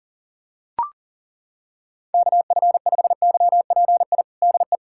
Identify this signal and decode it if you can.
Morse code, heard as a keyed tone, about 40 wpm, 700 Hz. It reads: KF5YPI DE
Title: USA Callsign DE Callsign 40wpm